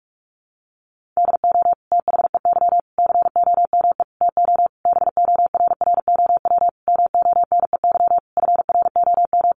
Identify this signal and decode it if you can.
Morse code, heard as a keyed tone, about 44 wpm, 700 Hz. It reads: DO THEY COME TO BORROW MONEY FROM
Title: Sentences from Top 1500 Words 45wpm